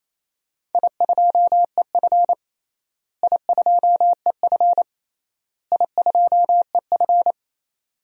Morse code, heard as a keyed tone, about 28 wpm, 700 Hz. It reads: I2EF I2EF I2EF
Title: DXCC-100 Call Signs - 4 Characters - Repeated 3x - 28wpm